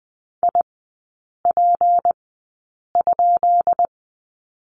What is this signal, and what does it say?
Morse code, heard as a keyed tone, about 20 wpm, spaced as for 8 wpm, 700 Hz. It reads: IP?
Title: 152 Lesson 25e - Letter p - Three Characters 20wpm